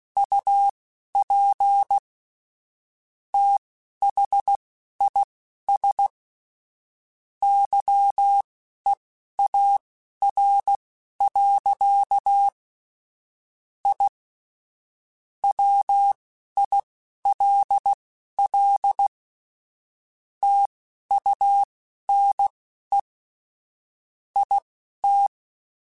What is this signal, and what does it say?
Morse code, heard as a keyed tone, about 16 wpm, spaced as for 12 wpm, 800 Hz. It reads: UP THIS YEAR. I WILL TUNE IT
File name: QSOs / QSOs-mp3s / WD0BGP 2@10